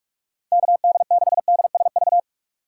Morse code, heard as a keyed tone, about 44 wpm, 700 Hz. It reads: KD6DSV
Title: USA Call Signs - 6 Characters - 6x Character Spacing - 45wpm